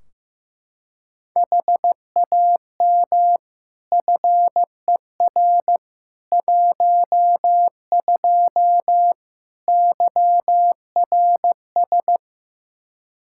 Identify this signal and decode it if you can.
Morse code, heard as a keyed tone, about 16 wpm, 700 Hz. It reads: HAM FER 12 YRS
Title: Practice QSO - Exchange 3 - More Info - 15wpm